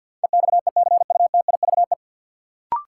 Morse code, heard as a keyed tone, about 50 wpm, 700 Hz. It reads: EXECUTIVE
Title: Sets of 5 Words Encore - New Word Drill - 50wpm